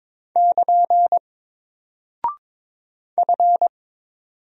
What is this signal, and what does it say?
Morse code, heard as a keyed tone, about 23 wpm, 700 Hz. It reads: <KN> F
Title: Single Character (Letter:Number:Punctuation:Prosign) 22wpm